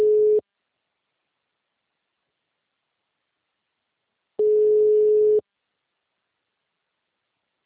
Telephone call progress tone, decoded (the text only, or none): ringback tone